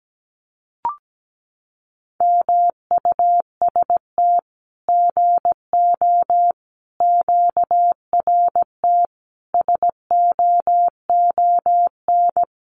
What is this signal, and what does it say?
Morse code, heard as a keyed tone, about 17 wpm, 700 Hz. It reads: MUST GO QRT SOON
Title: Common QSO Phrases 17wpm